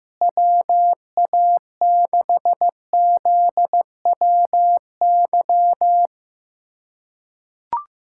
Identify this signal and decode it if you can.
Morse code, heard as a keyed tone, about 15 wpm, 700 Hz. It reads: WA6ZWY
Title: USA Callsign DE Callsign - 6 Characters - 15wpm